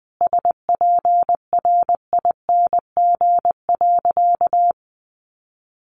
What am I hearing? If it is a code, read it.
Morse code, heard as a keyed tone, about 20 wpm, 700 Hz. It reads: SPRING.
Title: Sentences from Top 1200 Words 20wpm